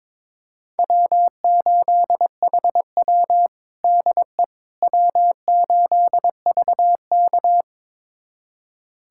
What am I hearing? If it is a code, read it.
Morse code, heard as a keyed tone, about 21 wpm, 700 Hz. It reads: W8HW DE W8VK
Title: USA Callsign DE Callsign - 4 Characters - 22wpm